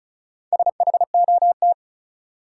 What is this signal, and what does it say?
Morse code, heard as a keyed tone, about 36 wpm, 700 Hz. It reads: SHOT
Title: Sentences from Top 1100 Words - New Word Drill - 35wpm